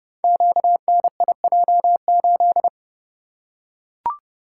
Morse code, heard as a keyed tone, about 30 wpm, 700 Hz. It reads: QNIJ8
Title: Code Groups - 5 Letter:Numbers - 30wpm